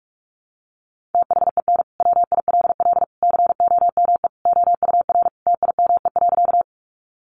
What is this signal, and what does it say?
Morse code, heard as a keyed tone, about 44 wpm, 700 Hz. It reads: THEN WILL COME OUR TIME.
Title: Sentences from Top 200 Words 45wpm